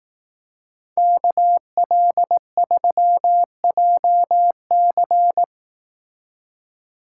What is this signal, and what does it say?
Morse code, heard as a keyed tone, about 18 wpm, 700 Hz. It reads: KL3JC
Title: Speed Racing - USA 2x2 Call Signs 15wpm